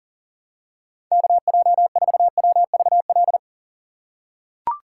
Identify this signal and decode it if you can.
Morse code, heard as a keyed tone, about 37 wpm, 700 Hz. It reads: KJ4WVL
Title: USA Call Signs - 6 Characters - 4x Character Spacing - 40wpm